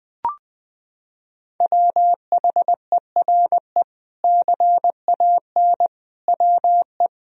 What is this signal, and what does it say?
Morse code, heard as a keyed tone, about 20 wpm, 700 Hz. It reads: WHERE CAN WE
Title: Sentences from Top 1000 Words - Review Part 2 - 20wpm